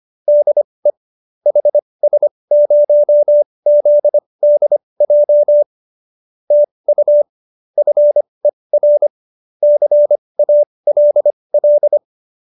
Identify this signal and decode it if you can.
Morse code, heard as a keyed tone, about 25 wpm, spaced as for 20 wpm, 590 Hz. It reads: DE HS0ZDJ TU FER CALL